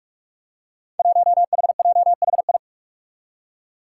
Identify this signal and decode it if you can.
Morse code, heard as a keyed tone, about 43 wpm, 700 Hz. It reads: 1HJHI
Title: Five Challenging Letter-Number Combinations 45wpm